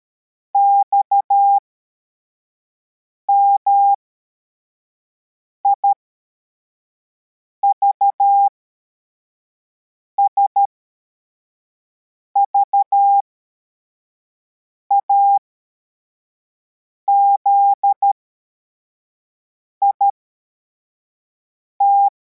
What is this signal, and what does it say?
Morse code, heard as a keyed tone, about 13 wpm, spaced as for 4 wpm, 800 Hz. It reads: XMIVS VAZIT
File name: formation / cw / Text55